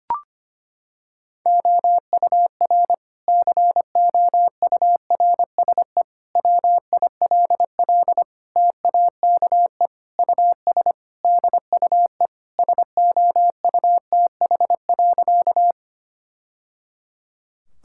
Morse code, heard as a keyed tone, about 25 wpm, 700 Hz. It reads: OUR COURSE WILL TAKE US DUE SOUTH.
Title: Sentences from Top 1000 Words - Review Part 6 - 25wpm